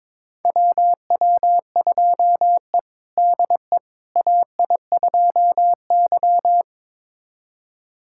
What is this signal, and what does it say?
Morse code, heard as a keyed tone, about 21 wpm, 700 Hz. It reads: WW2E DE AI2Y
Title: USA Callsign DE Callsign - 4 Characters - 22wpm